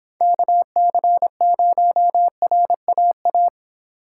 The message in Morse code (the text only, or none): KC0RAA